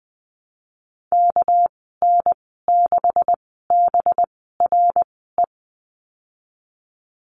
Morse code, heard as a keyed tone, about 20 wpm, spaced as for 15 wpm, 700 Hz. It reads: KN6BRE
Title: 207 Lesson 33g - Letter j - Call Signs 20wpm